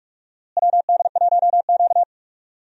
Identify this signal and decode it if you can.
Morse code, heard as a keyed tone, about 44 wpm, 700 Hz. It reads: WD1Q
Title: USA Call Signs - 4 Characters - Repeated 2x - 45wpm